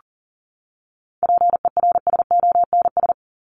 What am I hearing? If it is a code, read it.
Morse code, heard as a keyed tone, about 40 wpm, 700 Hz. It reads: PERSONS
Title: ES:S Suffix - Top 100 Words with Base Words 40wpm